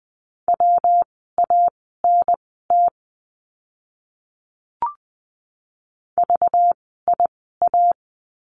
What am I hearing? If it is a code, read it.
Morse code, heard as a keyed tone, about 20 wpm, spaced as for 15 wpm, 700 Hz. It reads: WANT VIA
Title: 192 Lesson 31f - Letter v - Words 20wpm